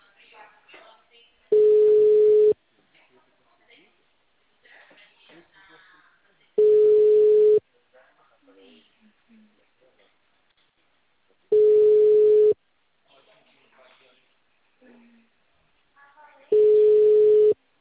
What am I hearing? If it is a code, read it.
Ringback tone (Europe, 425 Hz), with speech.